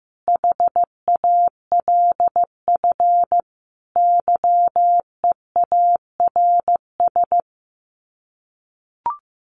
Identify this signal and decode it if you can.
Morse code, heard as a keyed tone, about 15 wpm, 700 Hz. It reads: HALF YEARS
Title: Sets of 5 Words Encore2 15wpm